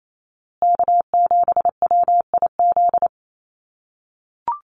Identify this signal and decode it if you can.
Morse code, heard as a keyed tone, about 28 wpm, 700 Hz. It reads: K7WIZ
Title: USA Call Signs Repeated 2x 28wpm